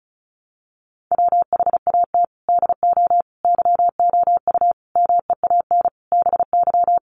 Morse code, heard as a keyed tone, about 34 wpm, 700 Hz. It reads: WHAT DO YOU MEAN BY
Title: Sentences from Top 1300 Words 35wpm